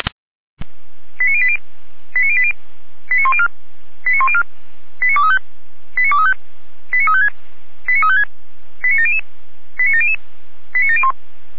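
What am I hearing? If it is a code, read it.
Five-tone selective call (ZVEI), heard as quick runs of five tones, tones of about 70 ms. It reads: 89080 89080 89104 89104 89235 89235 89356 89356 89700 89700 89701
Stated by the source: ZVEI five-tone selective call received on an SDR.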